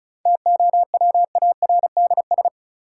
Morse code, heard as a keyed tone, about 36 wpm, 700 Hz. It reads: TOWARDS
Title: Sentences from Top 1000 Words - Review Part 2 - New Word Drill - 35wpm